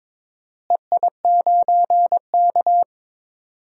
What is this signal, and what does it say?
Morse code, heard as a keyed tone, about 22 wpm, 700 Hz. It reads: EI9K
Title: DXCC-100 Call Signs - 4 Characters - Repeated 3x - 22wpm